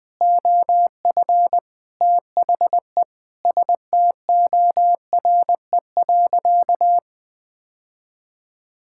Morse code, heard as a keyed tone, about 20 wpm, 700 Hz. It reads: OF THE STORE.
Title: Sentences from Top 1000 Words - Review Part 5 - 20wpm